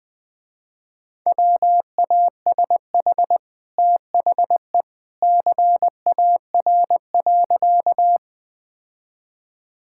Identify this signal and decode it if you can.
Morse code, heard as a keyed tone, about 20 wpm, 700 Hz. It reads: WASH THE CAR.
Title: Easy Sentences with Hints - 20wpm